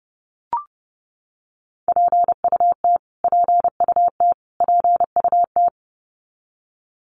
Morse code, heard as a keyed tone, about 30 wpm, 700 Hz. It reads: PUT PUT PUT
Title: Top 500 Words Repeated 30wpm